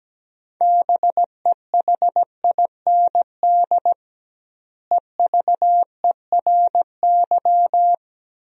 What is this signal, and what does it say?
Morse code, heard as a keyed tone, about 17 wpm, 700 Hz. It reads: BEHIND EVERY
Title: 2x Word-Spacing - Sentences from Top 500 Words - 17wpm